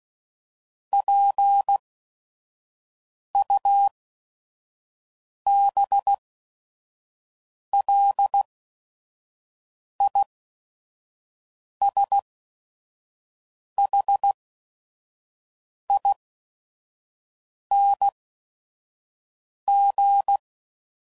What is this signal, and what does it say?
Morse code, heard as a keyed tone, about 16 wpm, spaced as for 5 wpm, 800 Hz. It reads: PUBLISHING
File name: Suffixes / Special ING's / VerySlowToFaster ING's